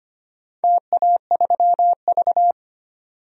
Morse code, heard as a keyed tone, about 25 wpm, 700 Hz. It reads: TA3V
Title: DXCC-100 Call Signs - 4 Characters - 25wpm